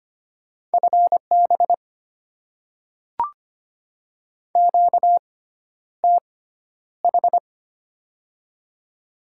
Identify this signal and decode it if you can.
Morse code, heard as a keyed tone, about 25 wpm, spaced as for 11 wpm, 700 Hz. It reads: FB QTH
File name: Top 20 QSO Elements - 6x Character Spacing - 25wpm